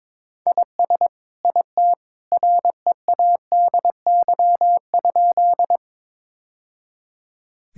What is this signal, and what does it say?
Morse code, heard as a keyed tone, about 22 wpm, 700 Hz. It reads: IS IT READY?
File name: practice / 22wpm / Sentences with 2-Word Phrases - 22wpm